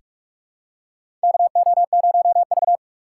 Morse code, heard as a keyed tone, about 46 wpm, 700 Hz. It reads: KO0V
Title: USA Call Signs - 4 Characters - 6x Character Spacing - 45wpm